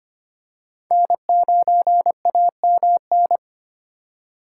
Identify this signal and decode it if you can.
Morse code, heard as a keyed tone, about 25 wpm, 700 Hz. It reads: N9AMN
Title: USA Call Signs - 5 Characters - 2x Character Spacing - 25wpm